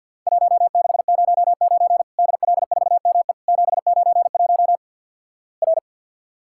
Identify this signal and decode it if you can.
Morse code, heard as a keyed tone, about 50 wpm, 700 Hz. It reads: 1609 DL4ME 791 R
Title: Worked All Europe DX Contest - QTC Exchange - 50wpm